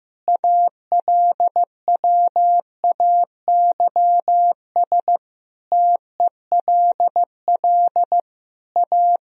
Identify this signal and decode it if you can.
Morse code, heard as a keyed tone, about 16 wpm, 700 Hz. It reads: ALWAYS TELL A
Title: Sentences from Top 1000 Words - Review Part 4 - 15wpm